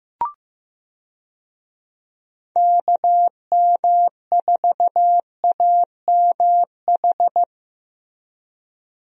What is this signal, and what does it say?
Morse code, heard as a keyed tone, about 15 wpm, 700 Hz. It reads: KM4AMH
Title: USA Call Signs - 6 Characters - Repeated 3x - 15wpm